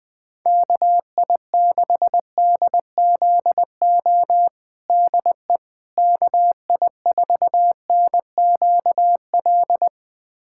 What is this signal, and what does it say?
Morse code, heard as a keyed tone, about 20 wpm, 700 Hz. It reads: KI6DZO DE KI4NQL
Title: USA Callsign DE Callsign 20wpm